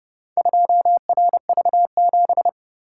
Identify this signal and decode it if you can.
Morse code, heard as a keyed tone, about 30 wpm, 700 Hz. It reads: 2RV7